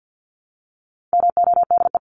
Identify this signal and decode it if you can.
Morse code, heard as a keyed tone, about 48 wpm, 700 Hz. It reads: MODE
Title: Top 1751-2000 Words 50wpm